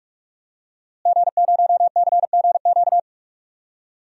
Morse code, heard as a keyed tone, about 46 wpm, 700 Hz. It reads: G0CGQ
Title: DXCC-100 Call Signs - 5 Characters - Repeated 3x - 45wpm